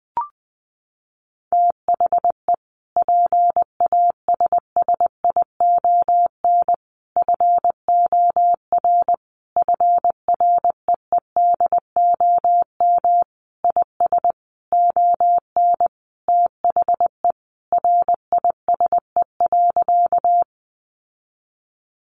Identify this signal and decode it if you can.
Morse code, heard as a keyed tone, about 20 wpm, 700 Hz. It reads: THE PASSION FOR FREEDOM IS ON THE RISE.
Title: Sentences from Top 1300 Words 20wpm